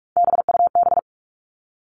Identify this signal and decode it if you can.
Morse code, heard as a keyed tone, about 44 wpm, 700 Hz. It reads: BUB
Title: All Cut Numbers - 3-Digit 45wpm